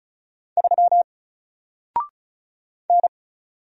Morse code, heard as a keyed tone, about 36 wpm, 700 Hz. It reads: 3 N
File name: Single-Letter-Number-35wpm